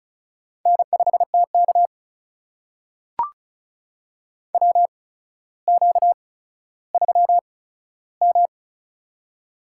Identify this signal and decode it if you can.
Morse code, heard as a keyed tone, about 34 wpm, spaced as for 15 wpm, 700 Hz. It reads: N5TK WQ3M
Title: USA Call Signs - 4 Characters - 8x Character Spacing - 35wpm